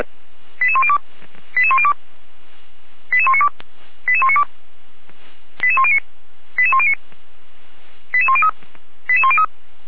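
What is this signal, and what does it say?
Five-tone selective call (ZVEI), heard as quick runs of five tones, tones of about 70 ms. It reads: 88102 88102 88192 88192 88109 88109 88103 88103
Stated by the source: ZVEI five-tone call received on SDR.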